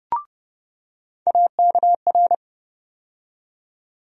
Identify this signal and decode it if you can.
Morse code, heard as a keyed tone, about 31 wpm, 700 Hz. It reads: AKR
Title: USA Call Sign Postfixes - 3 Letters 30wpm